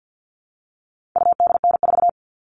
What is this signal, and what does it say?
Morse code, heard as a keyed tone, about 48 wpm, 700 Hz. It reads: UDN4
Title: All Cut Numbers - 4-Digit 50wpm